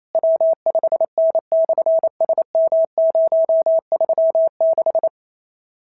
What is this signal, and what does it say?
Morse code, heard as a keyed tone, about 28 wpm, 650 Hz. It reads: W5N/SM036
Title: SOTA QSO Practice 28wpm